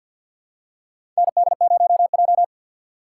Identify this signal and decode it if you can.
Morse code, heard as a keyed tone, about 48 wpm, 700 Hz. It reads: ND0J
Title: USA Call Signs - 4 Characters - 6x Character Spacing - 50wpm